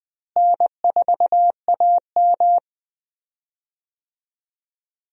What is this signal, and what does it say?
Morse code, heard as a keyed tone, about 20 wpm, 700 Hz. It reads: N4AM
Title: USA Call Signs - 4 Characters - Repeated 3x - 20wpm